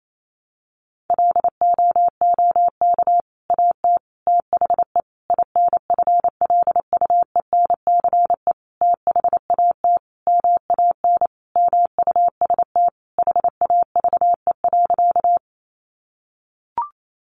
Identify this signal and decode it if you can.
Morse code, heard as a keyed tone, about 28 wpm, 700 Hz. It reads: LOOK AT THE INFLUENCE THAT MAN MUST HAVE.
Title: Sentences from Top 1100 Words 28wpm